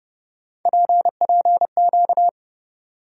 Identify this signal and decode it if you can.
Morse code, heard as a keyed tone, about 29 wpm, 700 Hz. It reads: PPQ
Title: USA Call Sign Postfixes - 3 Letters 30wpm